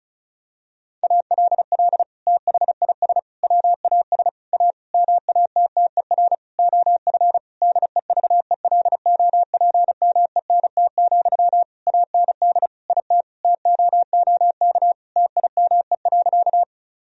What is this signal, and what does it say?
Morse code, heard as a keyed tone, about 36 wpm, 700 Hz. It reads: ALL THIS WAS A MATTER OF DEVELOPMENT, AND IT TOOK TIME.
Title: Sentences from Top 1000 Words - Review Part 4 - 35wpm